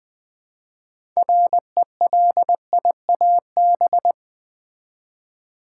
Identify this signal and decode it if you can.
Morse code, heard as a keyed tone, about 20 wpm, 700 Hz. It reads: RELIAB
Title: Build A Word - 3 Letters At A Time - 8-Letter Words - 20wpm